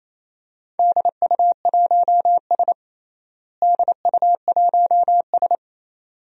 Morse code, heard as a keyed tone, about 28 wpm, 700 Hz. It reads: DU1S DU1S
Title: DXCC-100 Call Signs - 4 Characters - Repeated 2x - 28wpm